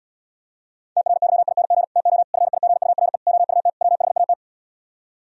Morse code, heard as a keyed tone, about 77 wpm, 700 Hz. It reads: TRYING TO FIGURE OUT WHAT